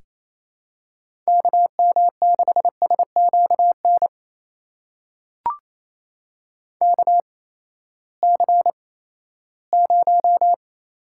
Morse code, heard as a keyed tone, about 28 wpm, spaced as for 16 wpm, 700 Hz. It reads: KM6SQN KC0
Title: USA Call Signs - 6 Characters - 8x Character Spacing - 28wpm